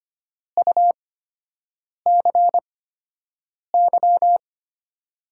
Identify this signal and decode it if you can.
Morse code, heard as a keyed tone, about 25 wpm, spaced as for 7 wpm, 700 Hz. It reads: UCY
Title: USA Call Signs - 5 Characters - 8x Character Spacing - 25wpm